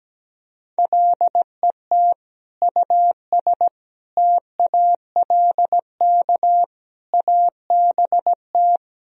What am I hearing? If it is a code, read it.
Morse code, heard as a keyed tone, about 17 wpm, 700 Hz. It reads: LET US TALK ABT
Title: Sentences from Top 400 Words and Abbreviations 17wpm